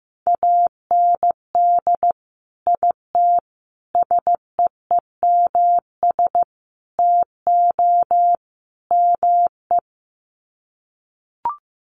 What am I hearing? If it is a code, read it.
Morse code, heard as a keyed tone, about 15 wpm, 700 Hz. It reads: AND IT SEEMS TO ME
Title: Sets of 5 Words Encore2 15wpm